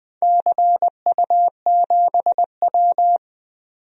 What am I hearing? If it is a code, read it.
Morse code, heard as a keyed tone, about 20 wpm, 700 Hz. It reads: CU7W
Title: DXCC-100 Call Signs - 4 Characters - 20wpm